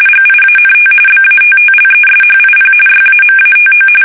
RTTY, about 45.45 baud. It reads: GIANT BATTLESHIP YAMATO
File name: fsk850 / battleshipnj